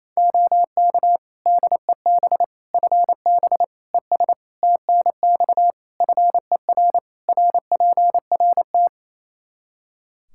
Morse code, heard as a keyed tone, about 28 wpm, 700 Hz. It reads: OK DEB FB ES TNX FER RPRT
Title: Practice QSO - Exchange 2 - Info - 28wpm